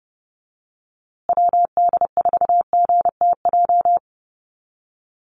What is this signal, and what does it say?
Morse code, heard as a keyed tone, about 30 wpm, 700 Hz. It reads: WD4GTJ
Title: USA Call Signs Repeated 2x 30wpm